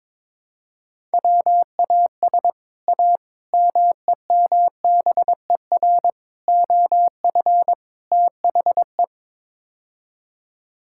Morse code, heard as a keyed tone, about 21 wpm, 700 Hz. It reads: WAS A MEMBER OF THE
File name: practice / 22wpm / Build A Phrase - 5-Word Phrases - 22wpm